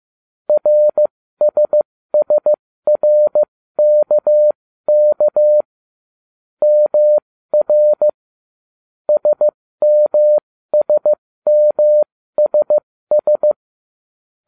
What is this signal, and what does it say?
Morse code, heard as a keyed tone, about 15 wpm, spaced as for 13 wpm, 600 Hz. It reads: RSSRKK MR SMSMSS